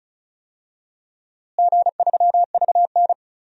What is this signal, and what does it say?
Morse code, heard as a keyed tone, about 36 wpm, 700 Hz. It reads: G3VN
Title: DXCC-100 Call Signs - 4 Characters - Repeated 2x - 35wpm